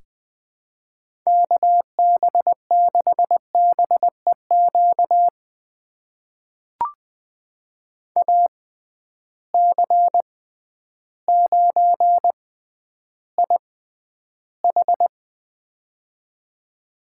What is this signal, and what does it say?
Morse code, heard as a keyed tone, about 20 wpm, spaced as for 11 wpm, 700 Hz. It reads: KB6BEQ AC9IH
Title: USA Call Signs - 6x Character Spacing - 20wpm